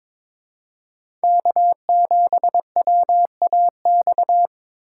Morse code, heard as a keyed tone, about 26 wpm, 700 Hz. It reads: K7WAX
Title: Speed Racing - USA 1x3 Call Signs 22wpm